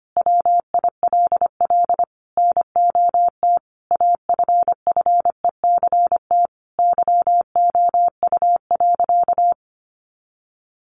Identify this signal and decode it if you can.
Morse code, heard as a keyed tone, about 25 wpm, 700 Hz. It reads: WILL NOT AFFECT YOU.
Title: Sentences from Top 900 Words 25wpm